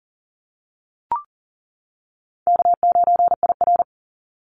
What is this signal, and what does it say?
Morse code, heard as a keyed tone, about 42 wpm, 700 Hz. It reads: K9IR
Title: NA SOTA Activator Callsigns 40wpm